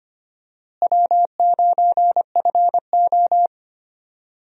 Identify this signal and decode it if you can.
Morse code, heard as a keyed tone, about 25 wpm, 700 Hz. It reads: W9FO
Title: USA Call Signs - 4 Characters - Repeated 3x - 25wpm